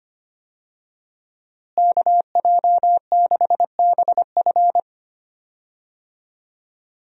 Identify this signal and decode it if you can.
Morse code, heard as a keyed tone, about 25 wpm, 700 Hz. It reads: KJ6BF
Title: USA Call Signs - 5 Characters - Repeated 2x - 25wpm